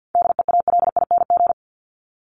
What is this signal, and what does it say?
Morse code, heard as a keyed tone, about 50 wpm, 700 Hz. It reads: DEALING
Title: ING Suffix - Top 100 Words with Base Words 50wpm